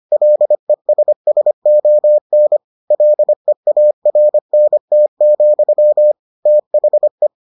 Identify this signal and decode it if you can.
Morse code, heard as a keyed tone, about 25 wpm, 600 Hz. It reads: LESSON LEARNT, THE